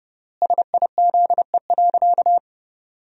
Morse code, heard as a keyed tone, about 30 wpm, 700 Hz. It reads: SIZE.
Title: 3x Word-Spacing - Sentences from Top 400 Words - 30wpm